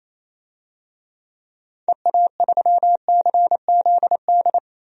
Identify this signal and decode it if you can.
Morse code, heard as a keyed tone, about 28 wpm, 700 Hz. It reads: EA3CZD
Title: DXCC-100 Call Signs - 6 Characters - 28wpm